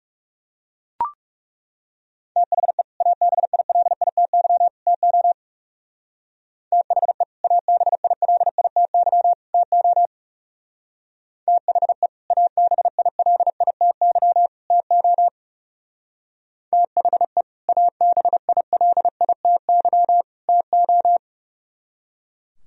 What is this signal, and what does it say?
Morse code, heard as a keyed tone, about 35 wpm, 700 Hz. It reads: THE ABILITY TO THE ABILITY TO THE ABILITY TO THE ABILITY TO
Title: Speed-Racing - Sets of 3 Words 30wpm